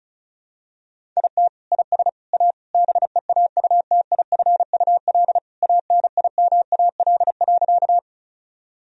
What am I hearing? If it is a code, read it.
Morse code, heard as a keyed tone, about 36 wpm, 700 Hz. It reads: IT IS A BEAUTIFUL ANIMAL.
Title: Sentences from Top 1000 Words - Review Part 1 - 35wpm